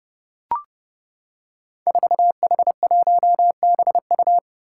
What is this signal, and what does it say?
Morse code, heard as a keyed tone, about 29 wpm, 700 Hz. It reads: 4H1BU
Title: Five Challenging Letter-Number Combinations 30wpm